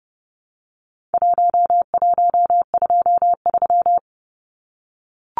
Morse code, heard as a keyed tone, about 30 wpm, 700 Hz. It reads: 1123
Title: Numbers - 4-Digit 30wpm